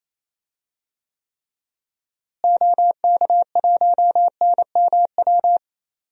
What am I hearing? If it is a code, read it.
Morse code, heard as a keyed tone, about 28 wpm, 700 Hz. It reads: OK1NMW
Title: DXCC-100 Call Signs - 6 Characters - Repeated 3x - 28wpm